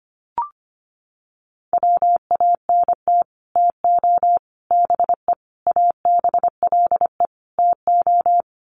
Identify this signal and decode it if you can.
Morse code, heard as a keyed tone, about 25 wpm, 700 Hz. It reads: WANT TO BE ABLE TO
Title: Sets of 5 Words 25wpm